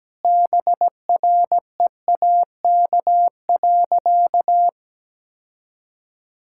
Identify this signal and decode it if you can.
Morse code, heard as a keyed tone, about 17 wpm, 700 Hz. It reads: BREAK.
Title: Sentences from Top 1000 Words - Review Part 6 - 17wpm